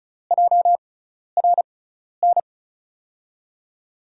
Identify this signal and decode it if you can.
Morse code, heard as a keyed tone, about 36 wpm, spaced as for 12 wpm, 700 Hz. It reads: JRN
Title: USA Call Signs - 5 Characters - 6x Character Spacing - 35wpm